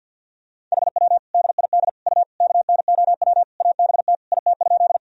Morse code, heard as a keyed tone, about 50 wpm, 700 Hz. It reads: HW DID U KNOW ABT IT?
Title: Sentences from Top 100 Words and Abbreviations 50wpm